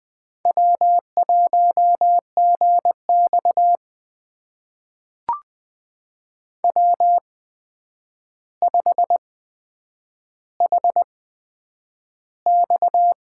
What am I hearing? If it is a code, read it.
Morse code, heard as a keyed tone, about 20 wpm, spaced as for 9 wpm, 700 Hz. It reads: W1GX W5HX
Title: USA Call Signs - 8x Character Spacing - 20wpm